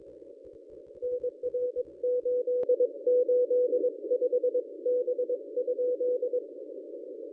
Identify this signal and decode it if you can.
Morse code, heard as a keyed tone, about 21 wpm, 500 Hz. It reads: NR 885 B?
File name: contests / ss-cw-k5tr-op-n5rz -1000